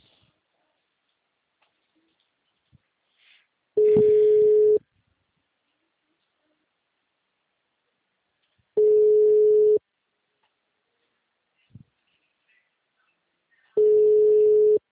Ringback tone (Europe, 425 Hz).